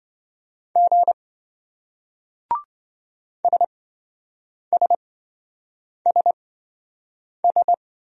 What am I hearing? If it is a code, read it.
Morse code, heard as a keyed tone, about 27 wpm, spaced as for 7 wpm, 700 Hz. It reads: G SSSS